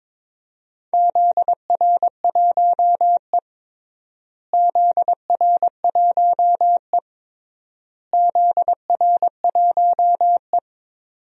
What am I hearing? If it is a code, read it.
Morse code, heard as a keyed tone, about 22 wpm, 700 Hz. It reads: ZR1E ZR1E ZR1E
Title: DXCC-100 Call Signs - 4 Characters - Repeated 3x - 22wpm